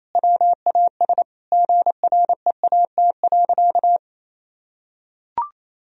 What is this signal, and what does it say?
Morse code, heard as a keyed tone, about 28 wpm, 700 Hz. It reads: WAS GREAT.
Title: Sentences from Top 1000 Words - Review Part 1 - 28wpm